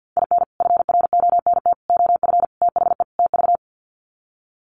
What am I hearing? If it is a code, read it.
Morse code, heard as a keyed tone, about 49 wpm, 700 Hz. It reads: IN FRONT OF THE TV
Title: Sets of 5 Words Encore2 50wpm